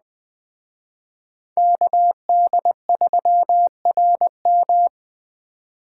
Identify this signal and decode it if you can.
Morse code, heard as a keyed tone, about 20 wpm, 700 Hz. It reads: KD3RM
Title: Speed Racing - USA 2x2 Call Signs 17wpm